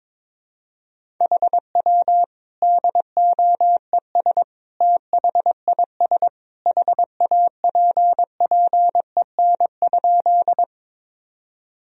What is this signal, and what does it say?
Morse code, heard as a keyed tone, about 22 wpm, 700 Hz. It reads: HW DOES THIS HAPPEN?
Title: Sentences from Top 400 Words and Abbreviations 22wpm